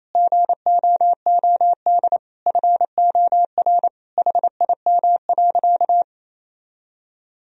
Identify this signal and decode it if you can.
Morse code, heard as a keyed tone, about 28 wpm, 700 Hz. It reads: GOOD FOR HIM.
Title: Sentences from Top 1000 Words - Review Part 1 - 28wpm